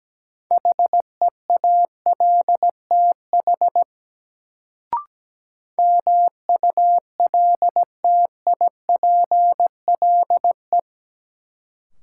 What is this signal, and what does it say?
Morse code, heard as a keyed tone, about 17 wpm, 700 Hz. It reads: HEALTH MULTIPLE
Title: Sets of 5 Words - New Word Drill - 17wpm